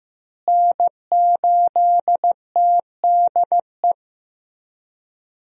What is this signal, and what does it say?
Morse code, heard as a keyed tone, about 15 wpm, 700 Hz. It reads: N8TDE
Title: USA Call Signs - 5 Characters - Repeated 3x - 15wpm